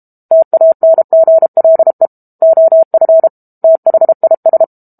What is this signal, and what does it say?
Morse code, heard as a keyed tone, about 32 wpm, 650 Hz. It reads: TANGLE OF THIS
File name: Book Two / 06-Chapter 6 - 32 WPM